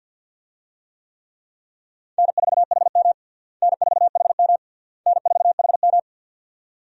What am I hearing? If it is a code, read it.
Morse code, heard as a keyed tone, about 51 wpm, 700 Hz. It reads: N4HM N4HM N4HM
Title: USA Call Signs - 4 Characters - Repeated 3x - 50wpm